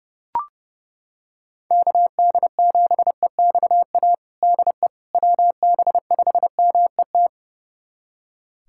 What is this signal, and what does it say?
Morse code, heard as a keyed tone, about 30 wpm, 700 Hz. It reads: KD7EXA DE WB5MET
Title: USA Callsign DE Callsign - 6 Characters - 30wpm